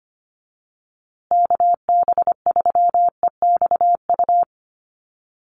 Morse code, heard as a keyed tone, about 25 wpm, 700 Hz. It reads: KB3EXU
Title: USA Call Signs Repeated 2x 25wpm